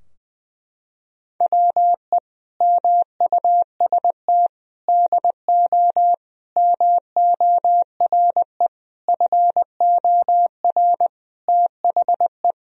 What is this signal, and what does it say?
Morse code, heard as a keyed tone, about 19 wpm, 700 Hz. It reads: WE MUST DO MORE FOR THE
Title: Sentences from Top 1000 Words - Review Part 3 - 20wpm